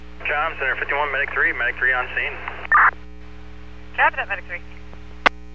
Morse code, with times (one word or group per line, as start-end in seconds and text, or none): none